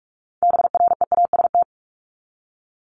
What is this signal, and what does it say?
Morse code, heard as a keyed tone, about 44 wpm, 700 Hz. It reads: BREAST
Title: Top 1751-2000 Words 45wpm